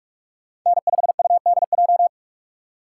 Morse code, heard as a keyed tone, about 44 wpm, 700 Hz. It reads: N5UDJ
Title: USA Call Signs - 5 Characters - 6x Character Spacing - 45wpm